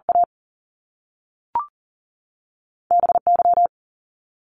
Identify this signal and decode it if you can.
Morse code, heard as a keyed tone, about 38 wpm, 700 Hz. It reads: A BY
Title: Top Two Letter Combinations in Words 40wpm